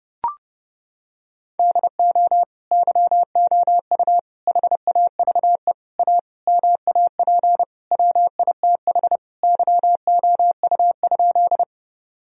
Morse code, heard as a keyed tone, about 32 wpm, 700 Hz. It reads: DO YOU HAVE A MAP WITH YOU?
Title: Sentences from Top 1000 Words - Review Part 2 - 30wpm